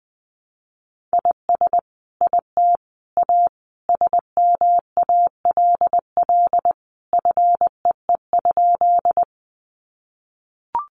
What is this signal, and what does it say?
Morse code, heard as a keyed tone, about 20 wpm, 700 Hz. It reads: IS IT A SMALL FEE?
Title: Sentences from Top 1500 Words 20wpm